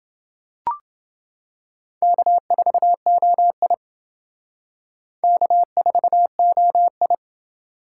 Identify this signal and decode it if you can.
Morse code, heard as a keyed tone, about 29 wpm, 700 Hz. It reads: K4OI K4OI